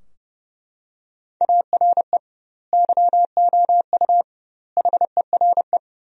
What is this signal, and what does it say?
Morse code, heard as a keyed tone, about 32 wpm, 700 Hz. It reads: ARE YOU HERE
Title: 2x Word-Spacing - Sentences from Top 400 Words - 30wpm